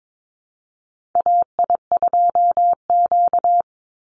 Morse code, heard as a keyed tone, about 22 wpm, 700 Hz. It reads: AI2Q
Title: Call Signs Repeated 3x - 22wpm